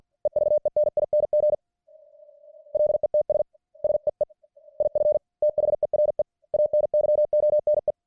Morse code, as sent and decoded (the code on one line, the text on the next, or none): . ...- . -. .. -. --. / .-.. - ... / ... . / .. ..-. / - .... .-. / .- -. -.-- --- -. .
EVENING LETS SEE IF THERE ANYONE